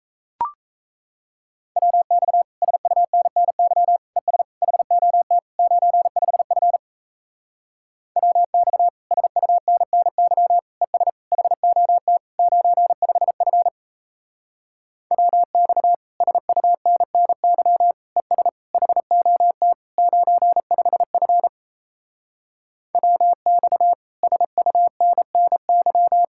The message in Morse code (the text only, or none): WX SUNNY ES HOT 95F WX SUNNY ES HOT 95F WX SUNNY ES HOT 95F WX SUNNY